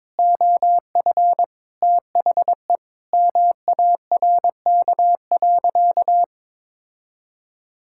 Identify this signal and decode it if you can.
Morse code, heard as a keyed tone, about 22 wpm, 700 Hz. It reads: OF THE MARK.
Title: Sentences from Top 1000 Words - Review Part 6 - 22wpm